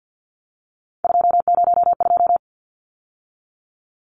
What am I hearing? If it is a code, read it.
Morse code, heard as a keyed tone, about 50 wpm, 700 Hz. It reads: 202
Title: Numbers - 3-Digit 50wpm